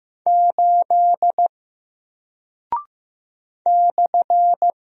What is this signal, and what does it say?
Morse code, heard as a keyed tone, about 15 wpm, 700 Hz. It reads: 8 /
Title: Single Character (Letter:Number:Punctuation:Prosign) Alt 15wpm